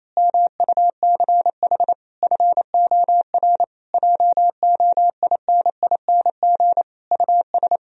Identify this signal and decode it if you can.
Morse code, heard as a keyed tone, about 26 wpm, 700 Hz. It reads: MUCH FOR JOINING US
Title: Speed-Racing - Sets of 5 Words 28wpm